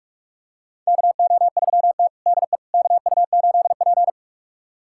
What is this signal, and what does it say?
Morse code, heard as a keyed tone, about 46 wpm, 700 Hz. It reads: KO3T DE KU8P
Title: USA Callsign DE Callsign - 4 Characters - 45wpm